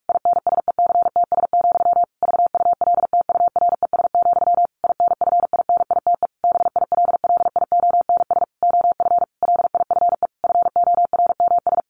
Morse code, heard as a keyed tone, about 44 wpm, 700 Hz. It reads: INSECTS, VULTURES, INFINITE BILLIONS OF LIFE FORMS
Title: A Sound Of Thunder 45wpm